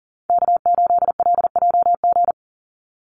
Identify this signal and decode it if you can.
Morse code, heard as a keyed tone, about 40 wpm, 700 Hz. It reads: K8LJG
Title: NA SOTA Chaser Callsigns 40wpm